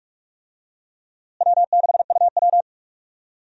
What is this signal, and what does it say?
Morse code, heard as a keyed tone, about 46 wpm, 700 Hz. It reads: W6UW
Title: USA Call Signs - 4 Characters - Repeated 2x - 45wpm